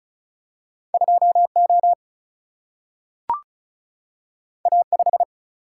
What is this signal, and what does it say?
Morse code, heard as a keyed tone, about 36 wpm, 700 Hz. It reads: 2O A5
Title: Single-Letter Single-Number 35wpm